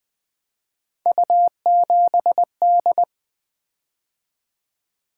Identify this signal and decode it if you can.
Morse code, heard as a keyed tone, about 20 wpm, 700 Hz. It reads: U7D
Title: Three Letter-Number Combinations - 20wpm